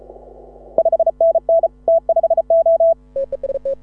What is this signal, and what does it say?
Morse code, heard as a keyed tone, about 33 wpm, 660 Hz. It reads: 5NN T5O
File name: AUDIO2018 / HA6OI CW